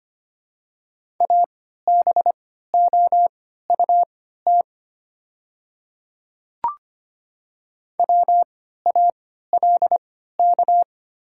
Morse code, heard as a keyed tone, about 25 wpm, spaced as for 14 wpm, 700 Hz. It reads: ABOUT WALK
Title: 3x Character Spacing - Top 300 Words - 25wpm